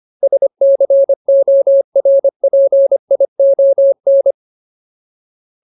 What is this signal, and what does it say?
Morse code, heard as a keyed tone, about 25 wpm, 550 Hz. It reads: SCORPION
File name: Ultimate Word List / 25WPM Ultimate